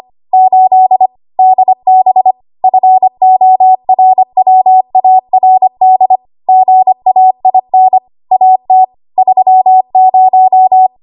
Morse code, heard as a keyed tone, about 25 wpm, 750 Hz. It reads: 8 DB FORWARD GAIN AT 30
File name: Archive / 25 WPM / 25WPM